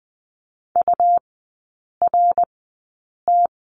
Morse code, heard as a keyed tone, about 20 wpm, spaced as for 8 wpm, 700 Hz. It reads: URT
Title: 100 Lesson 17e - Letter u - Three Characters 20wpm